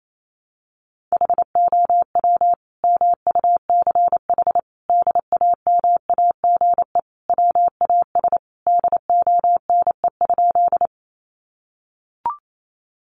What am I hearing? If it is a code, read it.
Morse code, heard as a keyed tone, about 28 wpm, 700 Hz. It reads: HOW MUCH DAMAGE WAS DONE?
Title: Sentences from Top 1100 Words 28wpm